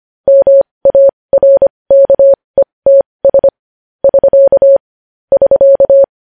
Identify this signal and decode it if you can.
Morse code, heard as a keyed tone, about 25 wpm, spaced as for 20 wpm, 550 Hz. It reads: MARKETS <SK> <SK>
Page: news headlines in morse code at 20 wpm